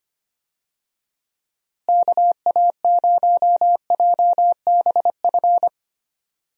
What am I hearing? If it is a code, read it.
Morse code, heard as a keyed tone, about 25 wpm, 700 Hz. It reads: KA0JBF
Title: USA Call Signs - 6 Characters - Repeated 3x - 25wpm